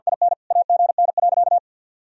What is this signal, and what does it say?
Morse code, heard as a keyed tone, about 52 wpm, 700 Hz. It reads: IN AGN.
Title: Sentences from Top 300 Words and Abbreviations 50wpm